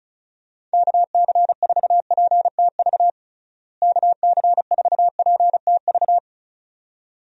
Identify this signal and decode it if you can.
Morse code, heard as a keyed tone, about 34 wpm, 700 Hz. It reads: KC4PTV KC4PTV
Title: USA Call Signs - 6 Characters - Repeated 2x - 35wpm